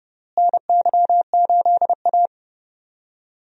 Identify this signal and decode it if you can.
Morse code, heard as a keyed tone, about 32 wpm, 700 Hz. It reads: NY8A
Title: USA Call Signs - 4 Characters - Repeated 3x - 30wpm